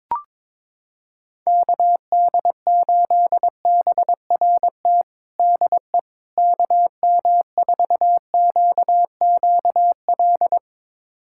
Morse code, heard as a keyed tone, about 22 wpm, 700 Hz. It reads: KD8BRT DE KM4QQL
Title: USA Callsign DE Callsign 22wpm